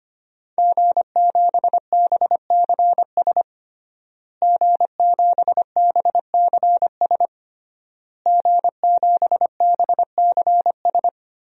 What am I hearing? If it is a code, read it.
Morse code, heard as a keyed tone, about 25 wpm, 700 Hz. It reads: G7BCS G7BCS G7BCS
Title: DXCC-100 Call Signs - 5 Characters - Repeated 3x - 25wpm